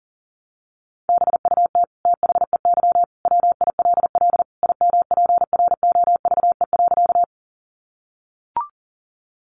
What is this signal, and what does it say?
Morse code, heard as a keyed tone, about 40 wpm, 700 Hz. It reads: BUT THEY WILL IMPROVE.
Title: Sentences from Top 800 Words 40wpm